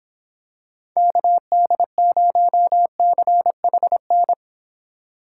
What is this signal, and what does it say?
Morse code, heard as a keyed tone, about 26 wpm, 700 Hz. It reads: KD0CHN